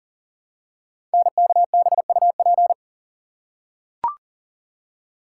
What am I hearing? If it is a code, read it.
Morse code, heard as a keyed tone, about 40 wpm, 700 Hz. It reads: NKBUP
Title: Code Groups - 5 Letters - 40wpm